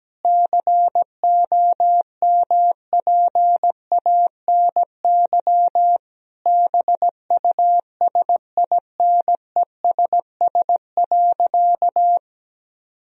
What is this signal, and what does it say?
Morse code, heard as a keyed tone, about 17 wpm, 700 Hz. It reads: COMPANY BUSINESS.
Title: Sentences from Top 1000 Words - Review Part 3 - 17wpm